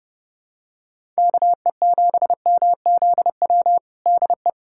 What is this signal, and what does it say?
Morse code, heard as a keyed tone, about 28 wpm, 700 Hz. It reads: KE7MZW DE
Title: USA Callsign DE Callsign - 6 Characters - 30wpm